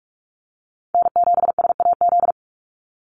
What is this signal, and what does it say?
Morse code, heard as a keyed tone, about 44 wpm, 700 Hz. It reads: N7SAZ
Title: USA Call Signs 45wpm